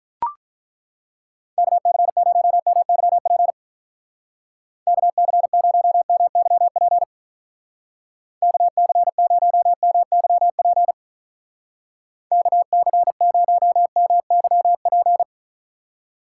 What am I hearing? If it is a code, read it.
Morse code, heard as a keyed tone, about 42 wpm, 700 Hz. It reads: KC0MYP KC0MYP KC0MYP KC0MYP
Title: Speed Racing - USA 2x3 Call Signs 35wpm